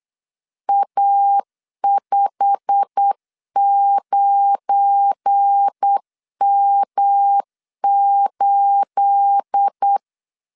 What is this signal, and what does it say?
Morse code, heard as a keyed tone, about 8 wpm, 800 Hz. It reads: A59M8
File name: CW / DET / 1 9 1500C S C 8wpm 5653